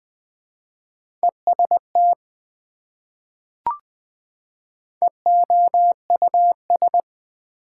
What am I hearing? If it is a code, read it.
Morse code, heard as a keyed tone, about 20 wpm, 700 Hz. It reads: EST EOUS
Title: Prefixes and Suffixes 20wpm